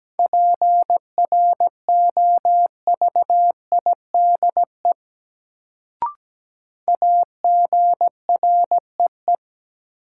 Morse code, heard as a keyed tone, about 17 wpm, 700 Hz. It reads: PROVIDE AGREE
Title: Sentences from Top 1100 Words - New Word Drill Alt - 17wpm